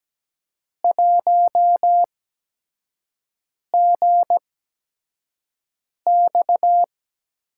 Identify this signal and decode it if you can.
Morse code, heard as a keyed tone, about 17 wpm, spaced as for 5 wpm, 700 Hz. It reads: 1GX
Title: USA Call Signs - 8x Character Spacing - 17wpm